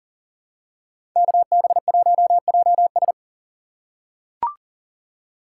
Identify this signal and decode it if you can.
Morse code, heard as a keyed tone, about 40 wpm, 690 Hz. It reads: KB1JS
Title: USA Call Signs - 2x Character Spacing - 40wpm